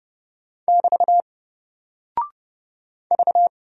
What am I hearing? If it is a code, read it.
Morse code, heard as a keyed tone, about 30 wpm, 700 Hz. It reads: <BT> V
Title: Single Character (Letter:Number:Punctuation:Prosign) 30wpm